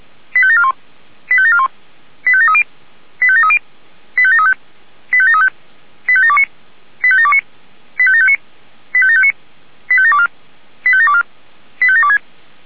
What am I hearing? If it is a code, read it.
Five-tone selective call (ZVEI), heard as quick runs of five tones, tones of about 70 ms. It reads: 85741 85741 85730 85730 85735 85735 85729 85729 85759 85759 85724 85724 85726